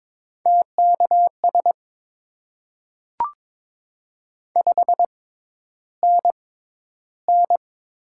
Morse code, heard as a keyed tone, about 22 wpm, spaced as for 11 wpm, 700 Hz. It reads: TKS 5NN
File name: Top 90 QSO Elements - 6x Character Spacing - 22wpm